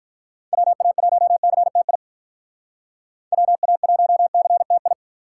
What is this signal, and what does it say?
Morse code, heard as a keyed tone, about 44 wpm, 700 Hz. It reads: WA1CTI WA1CTI
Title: Speed Racing - USA Call Signs 35wpm